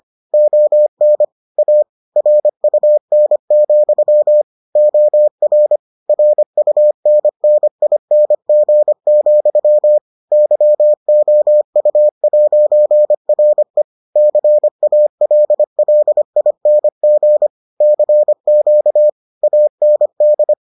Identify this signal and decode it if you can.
Morse code, heard as a keyed tone, about 25 wpm, 600 Hz. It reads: ON A RUN, OR RUNNING, YOU'RE CALLING CQ AND